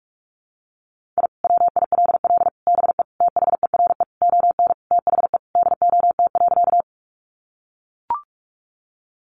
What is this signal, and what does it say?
Morse code, heard as a keyed tone, about 44 wpm, 700 Hz. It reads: I WILL BE THERE ON THE DOT.
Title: Sentences from Top 1500 Words 45wpm